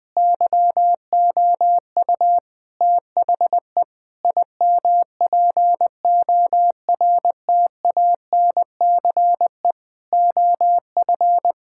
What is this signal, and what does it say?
Morse code, heard as a keyed tone, about 20 wpm, 700 Hz. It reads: YOU THE IMPORTANCE OF
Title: Sentences from Top 1600 Words 20wpm